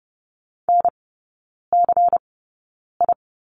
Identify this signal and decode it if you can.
Morse code, heard as a keyed tone, about 30 wpm, spaced as for 9 wpm, 700 Hz. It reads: NCI
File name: Lesson 16e - Letter c - Three Characters 30wpm